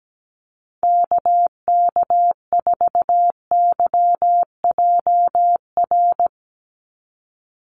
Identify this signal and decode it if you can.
Morse code, heard as a keyed tone, about 17 wpm, 700 Hz. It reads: KK4YJR
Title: USA Call Signs 17wpm